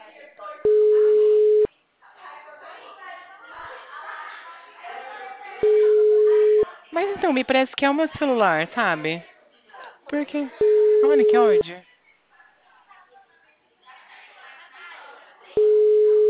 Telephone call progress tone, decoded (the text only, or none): ringback tone